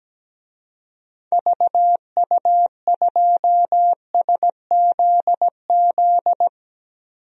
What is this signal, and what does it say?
Morse code, heard as a keyed tone, about 17 wpm, 700 Hz. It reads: VU2SZZ
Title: DXCC-100 Call Signs - 6 Characters - 17wpm